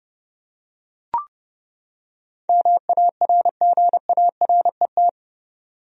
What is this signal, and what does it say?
Morse code, heard as a keyed tone, about 30 wpm, 700 Hz. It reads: MARGARET
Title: USA Common Names - 30wpm